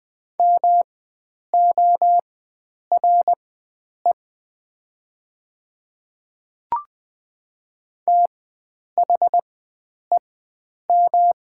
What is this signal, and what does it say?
Morse code, heard as a keyed tone, about 20 wpm, spaced as for 9 wpm, 700 Hz. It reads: MORE THEM
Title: 4x Character Spacing - Top 500 Words - 20wpm